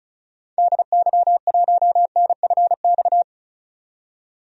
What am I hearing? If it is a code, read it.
Morse code, heard as a keyed tone, about 36 wpm, 700 Hz. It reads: DY1NFX
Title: DXCC-100 Call Signs - 6 Characters - 35wpm